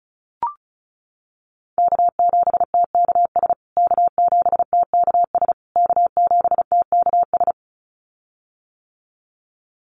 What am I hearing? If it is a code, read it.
Morse code, heard as a keyed tone, about 34 wpm, 700 Hz. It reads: K7TKS K7TKS K7TKS
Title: USA Call Signs Repeated 3x 35wpm